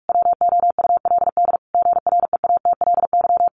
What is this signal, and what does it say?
Morse code, heard as a keyed tone, about 44 wpm, 700 Hz. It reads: WOULD GREATLY
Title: Sentences from Top 1400 Words 45wpm